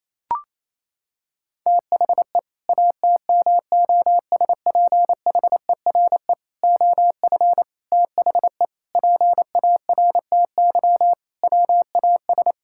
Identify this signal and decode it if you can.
Morse code, heard as a keyed tone, about 28 wpm, 700 Hz. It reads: THE ATMOSPHERE OF THE PARTY WAS
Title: Sentences from Top 1600 Words 28wpm